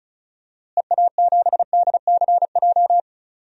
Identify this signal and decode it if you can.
Morse code, heard as a keyed tone, about 36 wpm, 700 Hz. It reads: EA7DCJ
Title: DXCC-100 Call Signs - 6 Characters - Repeated 2x - 35wpm